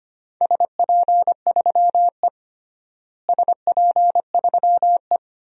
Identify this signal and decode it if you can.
Morse code, heard as a keyed tone, about 25 wpm, 700 Hz. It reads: SP3E SP3E
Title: DXCC-100 Call Signs - 4 Characters - Repeated 2x - 25wpm